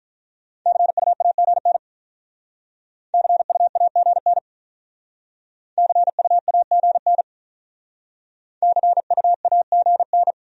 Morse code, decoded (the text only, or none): CUAGN CUAGN CUAGN CUAGN